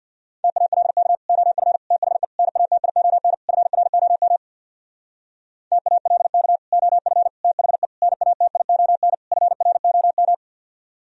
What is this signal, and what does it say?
Morse code, heard as a keyed tone, about 65 wpm, 700 Hz. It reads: TALK OF THE NATION FROM TALK OF THE NATION FROM